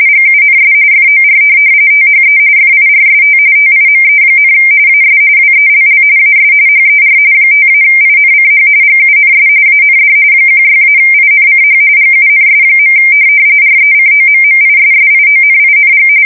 RTTY, about 45.45 baud. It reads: GUNS FOR THE FIRST TIME, AS WELL AS HER 5-INCH GUNS AND 16-INCH GUNS. RECOMMISSIONING
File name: fsk170 / USS-New-Jersey